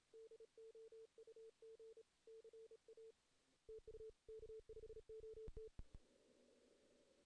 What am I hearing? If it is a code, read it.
Morse code, heard as a keyed tone, about 33 wpm, 440 Hz. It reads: DOUG CA TU K5OT
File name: 2016 JAN CW NAQP / 2016 Jan naqp k5ot - 0-2